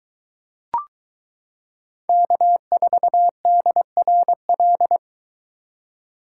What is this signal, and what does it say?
Morse code, heard as a keyed tone, about 22 wpm, 700 Hz. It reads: K4DRL